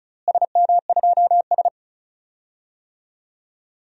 Morse code, heard as a keyed tone, about 36 wpm, 700 Hz. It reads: SM2S
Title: DXCC-100 Call Signs - 4 Characters - Repeated 2x - 35wpm